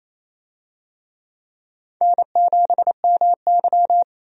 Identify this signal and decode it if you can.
Morse code, heard as a keyed tone, about 28 wpm, 700 Hz. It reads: N7MY
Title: USA Call Signs - 4 Characters - 28wpm